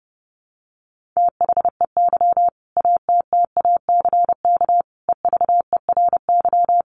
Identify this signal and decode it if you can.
Morse code, heard as a keyed tone, about 28 wpm, 700 Hz. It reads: THEY ATTACK EVERY
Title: Sentences from Top 1500 Words 30wpm